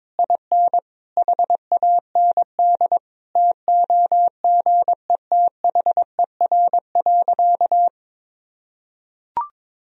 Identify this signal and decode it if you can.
Morse code, heard as a keyed tone, about 22 wpm, 700 Hz. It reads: IN HAND TOGETHER.
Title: Sentences from Top 1000 Words - Review Part 4 - 22wpm